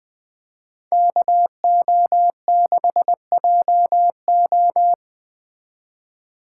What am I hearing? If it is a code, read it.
Morse code, heard as a keyed tone, about 20 wpm, 700 Hz. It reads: KO6JO
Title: Speed Racing - USA 2x2 Call Signs 17wpm